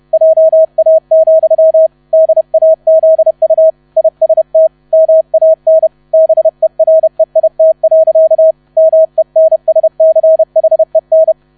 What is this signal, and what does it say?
Morse code, heard as a keyed tone, about 31 wpm, 640 Hz. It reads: JA, DAZU IST MAN BEREIT. MENSCHEN
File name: 30wpm